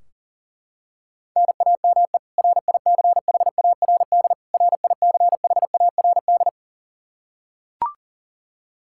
Morse code, heard as a keyed tone, about 37 wpm, 700 Hz. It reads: NAME RICHARD RICHARD
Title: Practice QSO - Exchange 1 - Beginning - 40wpm